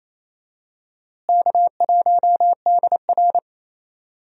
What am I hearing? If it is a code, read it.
Morse code, heard as a keyed tone, about 28 wpm, 700 Hz. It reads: K1DR
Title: USA Call Signs - 4 Characters - Repeated 2x - 28wpm